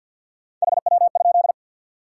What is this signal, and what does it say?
Morse code, heard as a keyed tone, about 53 wpm, 700 Hz. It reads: HW?
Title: Top 180 QSO Elements 50wpm